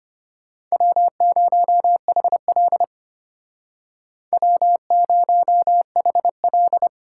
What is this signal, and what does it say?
Morse code, heard as a keyed tone, about 32 wpm, 700 Hz. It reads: W0HL W0HL